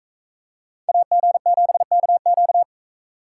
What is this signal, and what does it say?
Morse code, heard as a keyed tone, about 40 wpm, 700 Hz. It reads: AG7KQ